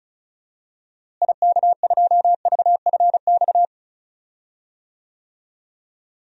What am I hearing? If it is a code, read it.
Morse code, heard as a keyed tone, about 36 wpm, 700 Hz. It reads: IK2VFX
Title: DXCC-100 Call Signs - 6 Characters - Repeated 2x - 35wpm